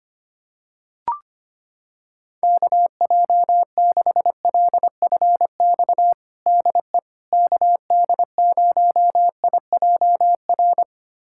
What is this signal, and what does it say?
Morse code, heard as a keyed tone, about 25 wpm, 700 Hz. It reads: KJ6LFX DE KD0IJR
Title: USA Callsign DE Callsign - 6 Characters - 25wpm